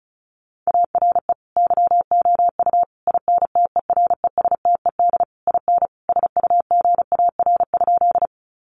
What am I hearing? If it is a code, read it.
Morse code, heard as a keyed tone, about 34 wpm, 700 Hz. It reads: ARE YOU INTERESTED IN SUGAR?
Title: Sentences from Top 1200 Words 35wpm